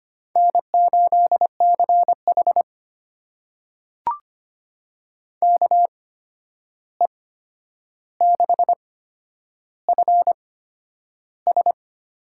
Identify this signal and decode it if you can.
Morse code, heard as a keyed tone, about 25 wpm, spaced as for 10 wpm, 700 Hz. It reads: N8CH KE6FS
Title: USA Call Signs - 8x Character Spacing - 25wpm